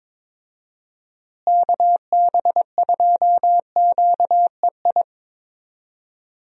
Morse code, heard as a keyed tone, about 22 wpm, 700 Hz. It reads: KB2QEI
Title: USA Call Signs - 6 Characters - Repeated 2x - 22wpm